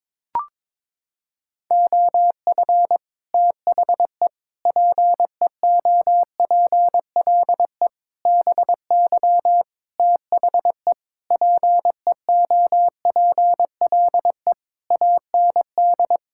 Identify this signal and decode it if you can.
Morse code, heard as a keyed tone, about 22 wpm, 700 Hz. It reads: OF THE PEOPLE BY THE PEOPLE AND
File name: Sets of 9 Words - 22wpm